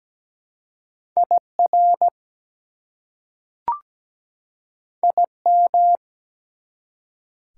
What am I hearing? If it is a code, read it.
Morse code, heard as a keyed tone, about 17 wpm, 700 Hz. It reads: IR IM
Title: Prefixes and Suffixes 17wpm